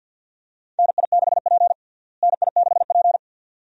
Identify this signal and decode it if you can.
Morse code, heard as a keyed tone, about 51 wpm, 700 Hz. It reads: NI6P NI6P
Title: USA Call Signs - 4 Characters - Repeated 2x - 50wpm